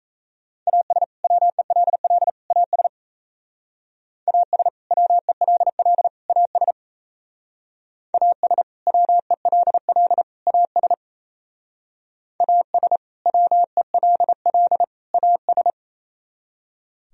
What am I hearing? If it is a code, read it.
Morse code, heard as a keyed tone, about 34 wpm, 700 Hz. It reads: AS WELL AS AS WELL AS AS WELL AS AS WELL AS
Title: Speed-Racing - Sets of 3 Words 28wpm